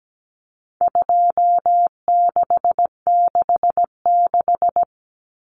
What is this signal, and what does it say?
Morse code, heard as a keyed tone, about 17 wpm, 700 Hz. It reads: 2666
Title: Common Cut Numbers - 4-Digit 17wpm